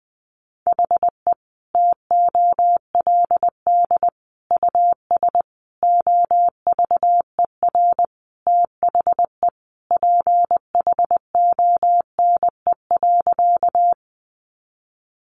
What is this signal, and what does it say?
Morse code, heard as a keyed tone, about 19 wpm, 700 Hz. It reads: HE TOLD US OVER THE PHONE.
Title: Sentences from Top 1100 Words 20wpm